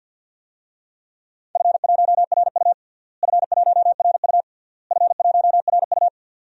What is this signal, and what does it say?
Morse code, heard as a keyed tone, about 52 wpm, 700 Hz. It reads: F1RU F1RU F1RU
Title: DXCC-100 Call Signs - 4 Characters - Repeated 3x - 50wpm